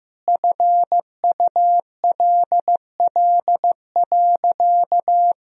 Morse code, heard as a keyed tone, about 15 wpm, 700 Hz. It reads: FULL.
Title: Sentences from Top 1000 Words - Review Part 3 - 15wpm